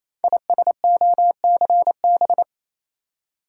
Morse code, heard as a keyed tone, about 27 wpm, 700 Hz. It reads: ISOCB
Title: Code Groups - 5 Letters - 28wpm